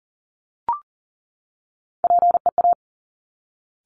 Morse code, heard as a keyed tone, about 40 wpm, 700 Hz. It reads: PEA
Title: Top Three Letter Combinations in Words 40wpm